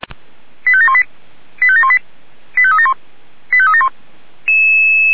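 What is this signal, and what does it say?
Five-tone selective call (ZVEI), heard as quick runs of five tones, tones of about 70 ms. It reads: 85718 85718 85371 85371
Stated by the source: ZVEI five-tone selective call received on SDR.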